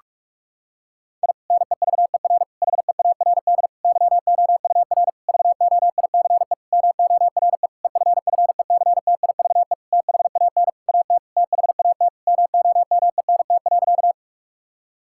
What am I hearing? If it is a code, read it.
Morse code, heard as a keyed tone, about 44 wpm, 700 Hz. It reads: I NEVER HEARD YOUR VOICE MORE EFFECTIVE THAN AT THAT MOMENT.
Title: Sentences from Top 1000 Words - Review Part 6 - 45wpm